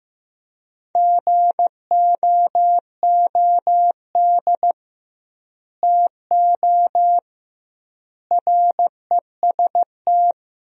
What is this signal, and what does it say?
Morse code, heard as a keyed tone, about 16 wpm, 700 Hz. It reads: GOOD TO REST
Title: 2x Word-Spacing - Sentences from Top 500 Words - 15wpm